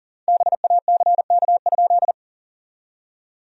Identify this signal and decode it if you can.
Morse code, heard as a keyed tone, about 37 wpm, 700 Hz. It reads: BACK?
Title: 4x Word-Spacing - Sentences from Top 400 Words - 40wpm